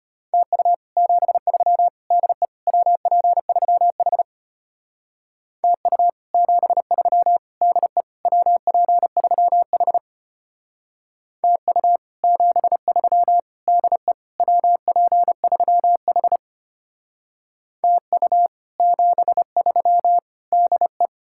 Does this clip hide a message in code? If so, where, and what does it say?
Yes: Morse code from 0.3 to 21.1 s, heard as a keyed tone, about 32 wpm, 700 Hz. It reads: TU 73 DE WP3H TU 73 DE WP3H TU 73 DE WP3H TU 73 DE